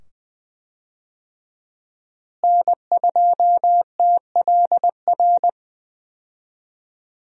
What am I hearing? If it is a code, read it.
Morse code, heard as a keyed tone, about 20 wpm, 700 Hz. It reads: N2TLR
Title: USA Call Signs - 5 Characters - Repeated 2x - 20wpm